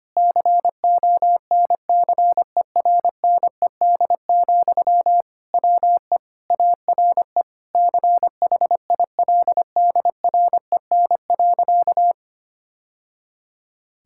Morse code, heard as a keyed tone, about 25 wpm, 700 Hz. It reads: CONCERNED, WE ARE CHILDREN.
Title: Sentences from Top 1000 Words - Review Part 6 - 25wpm